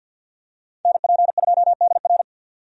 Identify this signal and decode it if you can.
Morse code, heard as a keyed tone, about 52 wpm, 700 Hz. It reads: NP2DR
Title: USA Call Signs - 5 Characters - 50wpm